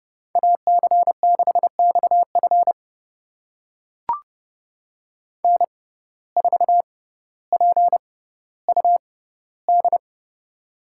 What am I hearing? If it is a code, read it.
Morse code, heard as a keyed tone, about 31 wpm, spaced as for 16 wpm, 700 Hz. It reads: AC6XF N4PUD
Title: USA Call Signs - 5 Characters - 6x Character Spacing - 30wpm